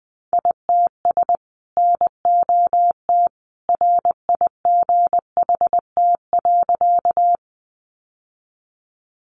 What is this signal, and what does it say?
Morse code, heard as a keyed tone, about 20 wpm, 700 Hz. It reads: ITS NOT RIGHT.
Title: Sentences from Top 600 Words 20wpm